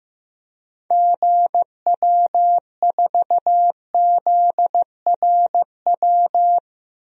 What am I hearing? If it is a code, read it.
Morse code, heard as a keyed tone, about 15 wpm, 700 Hz. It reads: GW4ZRW
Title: DXCC-100 Call Signs - 6 Characters - 15wpm